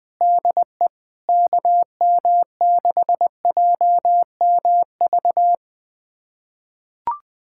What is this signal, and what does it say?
Morse code, heard as a keyed tone, about 21 wpm, 700 Hz. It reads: DE KM6JMV
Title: USA Callsign DE Callsign - 6 Characters - 20wpm